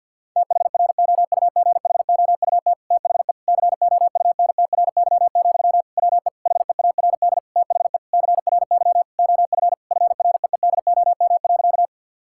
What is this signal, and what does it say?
Morse code, heard as a keyed tone, about 50 wpm, 700 Hz. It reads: THROUGHOUT THE COUNTRY, WE HEARD THE CRY OF FREEDOM.
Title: Sentences from Top 1000 Words - Review Part 5 - 50wpm